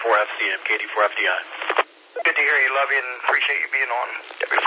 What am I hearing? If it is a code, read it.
Emergency Alert System (SAME) tones, heard as harsh buzzing bursts, with speech. It is heard from 1.1 to 3.9 s.